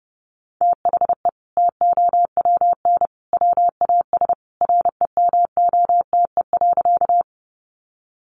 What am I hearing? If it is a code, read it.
Morse code, heard as a keyed tone, about 30 wpm, 700 Hz. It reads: THE TOWN WAS REMOTE.
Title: Sentences from Top 1300 Words 30wpm